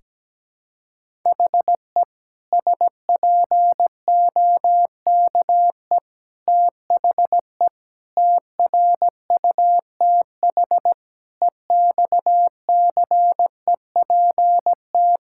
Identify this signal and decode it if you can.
Morse code, heard as a keyed tone, about 17 wpm, 700 Hz. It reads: HE SPOKE THE TRUTH EXCEPT
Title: Sentences from Top 1600 Words 17wpm